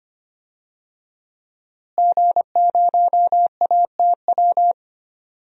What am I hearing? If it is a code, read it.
Morse code, heard as a keyed tone, about 25 wpm, 700 Hz. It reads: G0ATW
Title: DXCC-100 Call Signs - 5 Characters - Repeated 2x - 25wpm